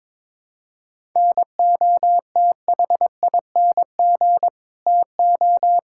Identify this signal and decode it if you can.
Morse code, heard as a keyed tone, about 22 wpm, 700 Hz. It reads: NOTHING TO
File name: practice / 22wpm / Build A Phrase - 5-Word Phrases - 22wpm